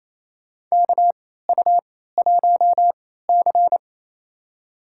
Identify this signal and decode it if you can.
Morse code, heard as a keyed tone, about 28 wpm, spaced as for 16 wpm, 700 Hz. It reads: KU1C
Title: USA Call Signs - 4 Characters - 3x Character Spacing - 28wpm